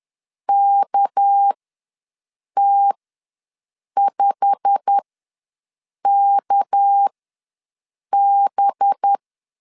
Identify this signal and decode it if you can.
Morse code, heard as a keyed tone, about 10 wpm, spaced as for 6 wpm, 800 Hz. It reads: KT5KB
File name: LEZIONI / 8 YJ45 / 8 YJ45 ws 10wpm 3843